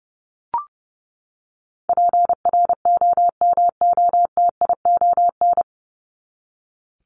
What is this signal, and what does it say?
Morse code, heard as a keyed tone, about 30 wpm, 700 Hz. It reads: PROMOTION
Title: Top 1501-1750 Words 30wpm